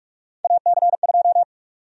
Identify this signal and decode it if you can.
Morse code, heard as a keyed tone, about 44 wpm, 700 Hz. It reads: AC2
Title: USA Call Sign Prefixes - 45wpm